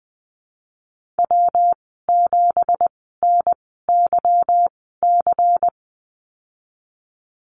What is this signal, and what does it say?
Morse code, heard as a keyed tone, about 20 wpm, spaced as for 15 wpm, 700 Hz. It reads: W7NYC
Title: 173 Lesson 28g - Number 9 - Call Signs 20wpm